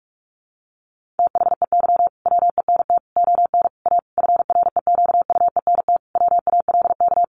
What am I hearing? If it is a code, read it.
Morse code, heard as a keyed tone, about 44 wpm, 700 Hz. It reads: THEY WENT ON A FREQUENT WALK
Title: Sentences from Top 1400 Words 45wpm